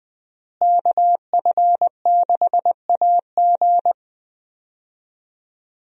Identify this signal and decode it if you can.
Morse code, heard as a keyed tone, about 21 wpm, 700 Hz. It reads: KF6AG
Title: USA Call Signs - 5 Characters - Repeated 2x - 20wpm